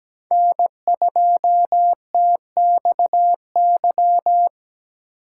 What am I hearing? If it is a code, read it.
Morse code, heard as a keyed tone, about 22 wpm, 700 Hz. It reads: N2TXY